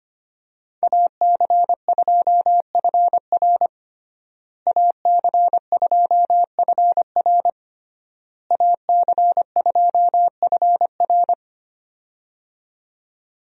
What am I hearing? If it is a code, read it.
Morse code, heard as a keyed tone, about 25 wpm, 700 Hz. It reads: AC2FR AC2FR AC2FR
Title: USA Call Signs - 5 Characters - Repeated 3x - 25wpm